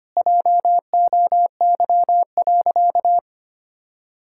Morse code, heard as a keyed tone, about 25 wpm, 700 Hz. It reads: JOY.
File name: practice / 25wpm / Sentences with 2-Word Phrases - 25wpm